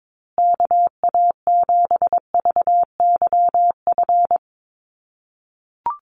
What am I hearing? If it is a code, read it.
Morse code, heard as a keyed tone, about 22 wpm, 700 Hz. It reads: KA7VYF
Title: USA Call Signs Repeated 3x 22wpm